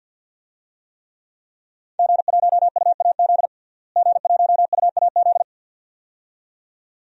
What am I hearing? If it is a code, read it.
Morse code, heard as a keyed tone, about 51 wpm, 700 Hz. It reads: G1UAZ G1UAZ
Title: DXCC-100 Call Signs - 5 Characters - Repeated 2x - 50wpm